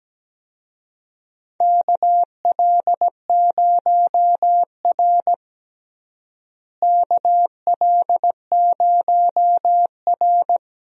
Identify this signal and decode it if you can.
Morse code, heard as a keyed tone, about 17 wpm, 700 Hz. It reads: KL0R KL0R
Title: USA Call Signs - 4 Characters - Repeated 2x - 17wpm